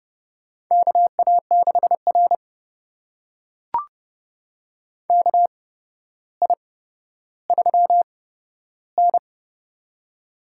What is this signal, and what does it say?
Morse code, heard as a keyed tone, about 29 wpm, spaced as for 13 wpm, 700 Hz. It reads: KA6R KI3N
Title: USA Call Signs - 4 Characters - 8x Character Spacing - 30wpm